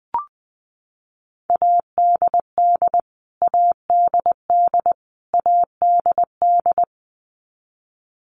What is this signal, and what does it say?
Morse code, heard as a keyed tone, about 20 wpm, 700 Hz. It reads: ADD ADD ADD
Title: Top 300 Words Repeated 20wpm